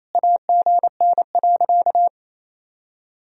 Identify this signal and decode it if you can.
Morse code, heard as a keyed tone, about 28 wpm, 700 Hz. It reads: AGN.
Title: Sentences from Top 500 Words and Abbreviations 28wpm